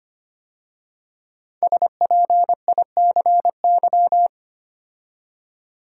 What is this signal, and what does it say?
Morse code, heard as a keyed tone, about 25 wpm, 700 Hz. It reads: SPICY
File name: 5 Letter Words - First 3 Letters Given - 25wpm